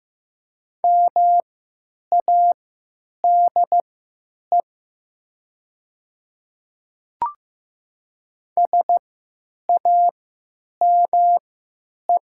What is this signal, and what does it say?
Morse code, heard as a keyed tone, about 15 wpm, spaced as for 9 wpm, 700 Hz. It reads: MADE SAME
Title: 3x Character Spacing - Top 200 Words - 15wpm